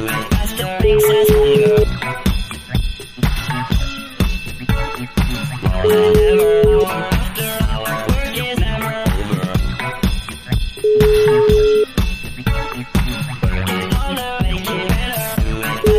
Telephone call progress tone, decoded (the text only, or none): ringback tone